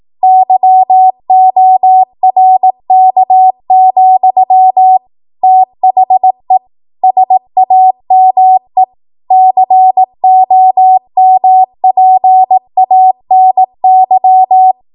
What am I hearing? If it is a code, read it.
Morse code, heard as a keyed tone, about 18 wpm, 750 Hz. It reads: YORK, THE SAME COMPANY